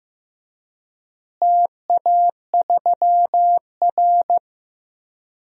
Morse code, heard as a keyed tone, about 15 wpm, 700 Hz. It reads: TA3R
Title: DXCC-100 Call Signs - 4 Characters - Repeated 2x - 15wpm